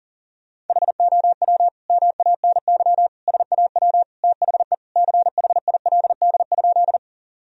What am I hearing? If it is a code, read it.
Morse code, heard as a keyed tone, about 40 wpm, 700 Hz. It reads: HOW MANY SAW THE CHILD?
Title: Sentences from Top 1000 Words - Review Part 1 - 40wpm